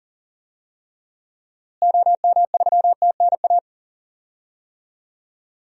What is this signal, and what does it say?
Morse code, heard as a keyed tone, about 40 wpm, 700 Hz. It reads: OM3TNA
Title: DXCC-100 Call Signs - 6 Characters - Repeated 3x - 40wpm